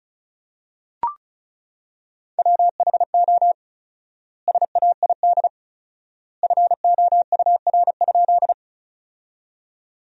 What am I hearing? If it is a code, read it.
Morse code, heard as a keyed tone, about 34 wpm, 700 Hz. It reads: WHO SAID FOUR?
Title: 4x Word-Spacing - Sentences from Top 300 Words - 35wpm